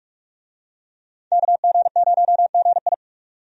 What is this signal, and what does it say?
Morse code, heard as a keyed tone, about 46 wpm, 700 Hz. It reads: KG0GI
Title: USA Call Signs - 5 Characters - 45wpm